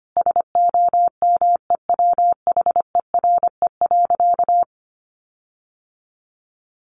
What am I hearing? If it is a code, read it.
Morse code, heard as a keyed tone, about 25 wpm, 700 Hz. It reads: SOMEWHERE.
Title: Sentences from Top 1400 Words 25wpm